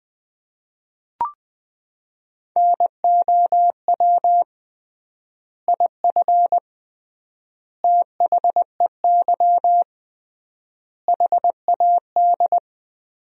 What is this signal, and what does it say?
Morse code, heard as a keyed tone, about 20 wpm, 700 Hz. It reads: NOW IF THEY HAD
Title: 3x Word-Spacing - Sentences from Top 500 Words - 20wpm